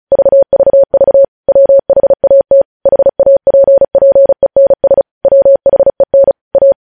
Morse code, heard as a keyed tone, about 36 wpm, 550 Hz. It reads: VVV WHAT HAPPENS WHEN A
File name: podcast / News headlines-1x-35x35